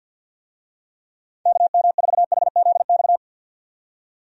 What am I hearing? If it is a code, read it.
Morse code, heard as a keyed tone, about 50 wpm, 700 Hz. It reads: KM4HZX
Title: USA Call Signs - 6 Characters - 50wpm